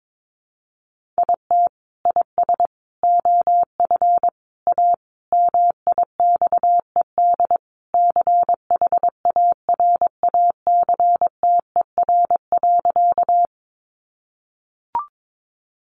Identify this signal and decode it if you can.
Morse code, heard as a keyed tone, about 22 wpm, 700 Hz. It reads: IT IS OF A MIXED CHARACTER.
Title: Sentences from Top 1500 Words 22wpm